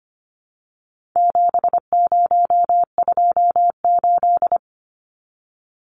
Morse code, heard as a keyed tone, about 25 wpm, 700 Hz. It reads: 7028
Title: Numbers - 4-Digit 25wpm